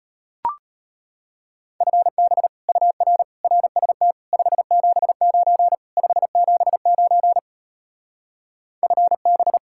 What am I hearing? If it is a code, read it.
Morse code, heard as a keyed tone, about 33 wpm, 700 Hz. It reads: FB UR RST 579 579 FB